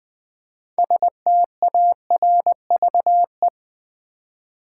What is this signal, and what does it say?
Morse code, heard as a keyed tone, about 20 wpm, 700 Hz. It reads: STARVE
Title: Build A Word - 3 Letters At A Time - 6-Letter Words - 20wpm